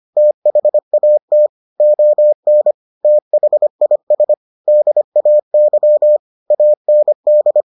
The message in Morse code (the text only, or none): THAT ON THIS DAY AND